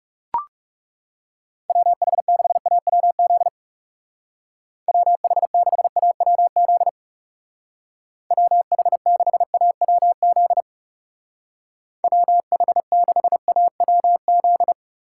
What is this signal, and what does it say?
Morse code, heard as a keyed tone, about 37 wpm, 700 Hz. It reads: WH6AWZ WH6AWZ WH6AWZ WH6AWZ
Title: Speed Racing - USA 2x3 Call Signs 30wpm